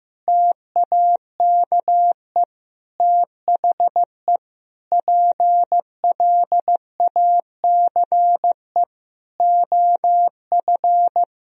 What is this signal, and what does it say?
Morse code, heard as a keyed tone, about 15 wpm, 700 Hz. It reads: TAKE THE PLACE OF
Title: Sentences from Top 1000 Words - Review Part 4 - 15wpm